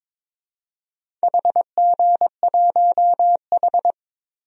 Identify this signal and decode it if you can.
Morse code, heard as a keyed tone, about 22 wpm, 700 Hz. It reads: HG1H
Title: DXCC-100 Call Signs - 4 Characters - 22wpm